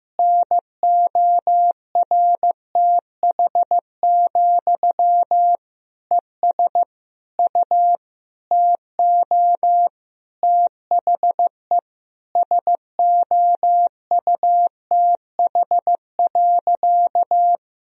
Morse code, heard as a keyed tone, about 15 wpm, 700 Hz. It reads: NORTH, ES U TO THE SOUTH.
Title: Sentences from Top 400 Words and Abbreviations 15wpm